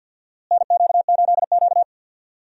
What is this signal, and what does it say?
Morse code, heard as a keyed tone, about 47 wpm, 690 Hz. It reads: NQ8Q
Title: USA Call Signs - 4 Characters - Repeated 2x - 50wpm